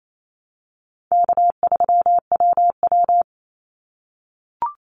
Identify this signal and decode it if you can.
Morse code, heard as a keyed tone, about 28 wpm, 700 Hz. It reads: K3WW
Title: Call Signs - 28wpm